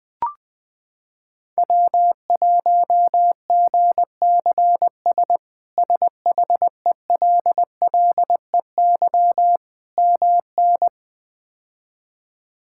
Morse code, heard as a keyed tone, about 20 wpm, 700 Hz. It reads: W1GCS SHELLEY MN
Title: K1USN SST QSO Practice - 20wpm